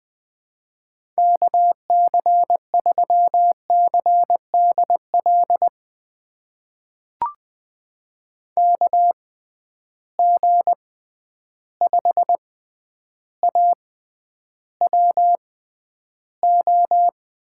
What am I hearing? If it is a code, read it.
Morse code, heard as a keyed tone, about 20 wpm, spaced as for 10 wpm, 700 Hz. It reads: KC3CDL KG5AWO
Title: USA Call Signs - 6 Characters - 6x Character Spacing - 20wpm